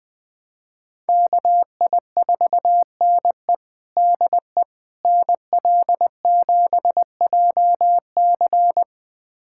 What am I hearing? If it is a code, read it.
Morse code, heard as a keyed tone, about 20 wpm, 700 Hz. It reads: KI4NE DE NL7JC
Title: USA Callsign DE Callsign 20wpm